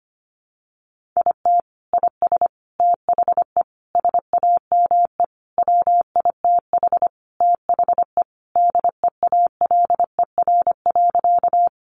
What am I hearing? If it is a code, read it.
Morse code, heard as a keyed tone, about 25 wpm, 700 Hz. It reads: IT IS THE SAME WITH THE DEALER.
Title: Sentences from Top 1100 Words 25wpm